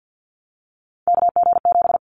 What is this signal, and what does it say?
Morse code, heard as a keyed tone, about 50 wpm, 700 Hz. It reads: KG7
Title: Callsign Prefixes for Top 25 DXCC - Repeated 3x - 50wpm